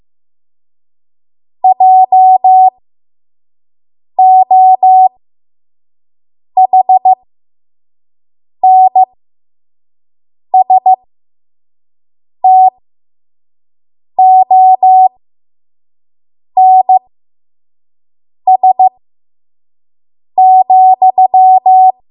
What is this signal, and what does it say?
Morse code, heard as a keyed tone, about 15 wpm, spaced as for 5 wpm, 750 Hz. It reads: JOHNSTONS,
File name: Morse / 05WPM